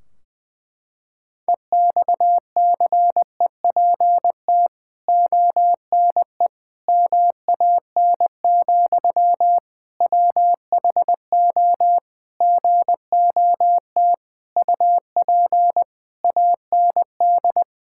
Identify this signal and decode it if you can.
Morse code, heard as a keyed tone, about 20 wpm, 700 Hz. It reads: EXCEPT ONE MAN, WHO GOT UP AND
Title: Sentences from Top 1600 Words 20wpm